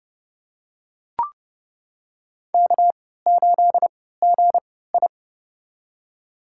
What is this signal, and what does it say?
Morse code, heard as a keyed tone, about 30 wpm, spaced as for 17 wpm, 700 Hz. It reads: K8GI
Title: USA Call Signs - 4 Characters - 3x Character Spacing - 30wpm